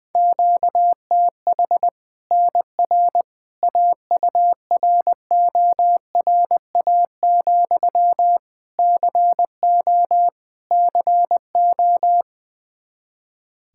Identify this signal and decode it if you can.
Morse code, heard as a keyed tone, about 20 wpm, 700 Hz. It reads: QTH NR AURORA, CO CO
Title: Practice QSO - Exchange 1 - Beginning - 20wpm